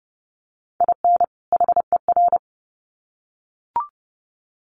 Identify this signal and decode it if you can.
Morse code, heard as a keyed tone, about 30 wpm, 700 Hz. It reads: IN HER
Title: Sets of 2 Words 30wpm